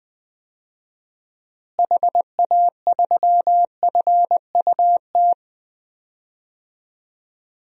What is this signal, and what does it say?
Morse code, heard as a keyed tone, about 20 wpm, 700 Hz. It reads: HA3FUT
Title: DXCC-100 Call Signs - 6 Characters - Repeated 2x - 20wpm